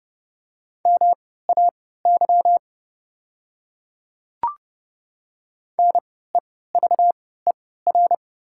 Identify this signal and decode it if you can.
Morse code, heard as a keyed tone, about 29 wpm, spaced as for 17 wpm, 690 Hz. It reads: MAY NEVER
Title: 3x Character Spacing - Top 300 Words - 30wpm